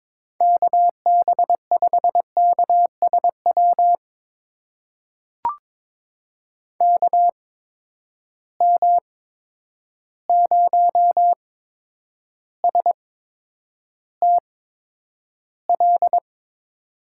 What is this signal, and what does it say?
Morse code, heard as a keyed tone, about 22 wpm, spaced as for 9 wpm, 700 Hz. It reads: KB5KSW KM0STL
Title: USA Call Signs - 6 Characters - 8x Character Spacing - 22wpm